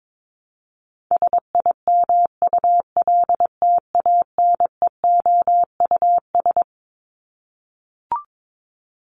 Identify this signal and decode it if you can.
Morse code, heard as a keyed tone, about 22 wpm, 700 Hz. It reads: SIMULTANEOUS
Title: LY Suffix - Top 100 Words with Base Words 22wpm